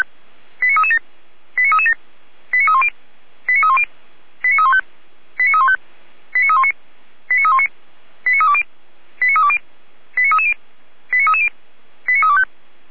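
Five-tone selective call (ZVEI), heard as quick runs of five tones, tones of about 70 ms. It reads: 89337 89337 89310 89310 89315 89315 89319 89319 89320 89320 89330 89330 89325
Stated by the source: ZVEI five-tone selective call 89304 received on an SDR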